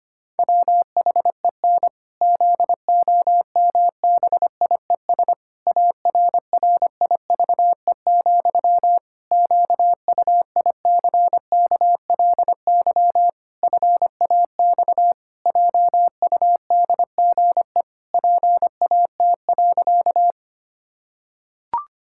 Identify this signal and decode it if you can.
Morse code, heard as a keyed tone, about 25 wpm, 700 Hz. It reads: WHEN ZOMBIES ARRIVE, QUICKLY FAX JUDGE PAT.
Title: Pangram Challenge v2 - 25wpm